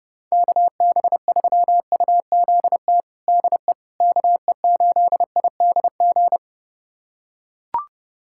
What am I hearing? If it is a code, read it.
Morse code, heard as a keyed tone, about 32 wpm, 700 Hz. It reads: KB3UZT DE KE8IDG
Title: USA Callsign DE Callsign 30wpm